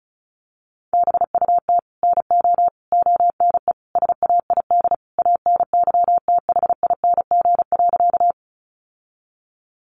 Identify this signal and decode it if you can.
Morse code, heard as a keyed tone, about 34 wpm, 700 Hz. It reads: BUT NO ONE SAID ANYTHING.
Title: Sentences from Top 1000 Words 35wpm